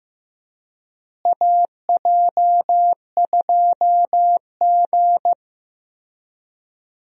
Morse code, heard as a keyed tone, about 15 wpm, 700 Hz. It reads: AJ2G
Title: USA Call Signs - 4 Characters - Repeated 2x - 15wpm